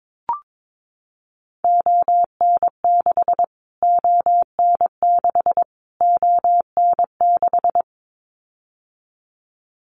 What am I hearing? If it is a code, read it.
Morse code, heard as a keyed tone, about 22 wpm, 700 Hz. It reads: ON6 ON6 ON6
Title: Callsign Prefixes for Top 25 DXCC - Repeated 3x - 22wpm